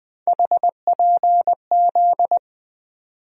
Morse code, heard as a keyed tone, about 21 wpm, 700 Hz. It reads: HPZ
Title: USA Call Sign Postfixes - 3 Letters 20wpm